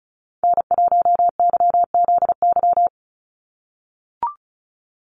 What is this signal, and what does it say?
Morse code, heard as a keyed tone, about 34 wpm, 700 Hz. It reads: N1YZY
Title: USA Call Signs 35wpm